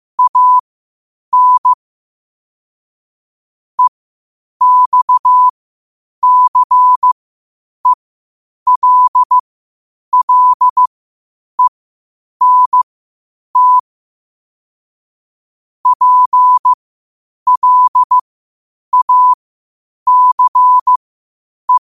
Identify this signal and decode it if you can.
Morse code, heard as a keyed tone, about 15 wpm, spaced as for 8 wpm, 1000 Hz. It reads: AN EXCELLENT PLACE
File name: qotdfiles / 8wpm